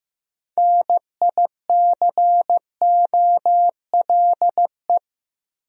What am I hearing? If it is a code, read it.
Morse code, heard as a keyed tone, about 15 wpm, 700 Hz. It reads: NICOLE
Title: USA Common Names - 15wpm